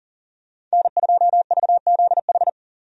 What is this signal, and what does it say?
Morse code, heard as a keyed tone, about 37 wpm, 700 Hz. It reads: N2VZH
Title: USA Call Signs - 5 Characters - 40wpm